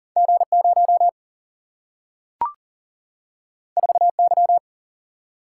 Morse code, heard as a keyed tone, about 40 wpm, 700 Hz. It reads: G 0 4 Y
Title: Single-Letter Single-Number 40wpm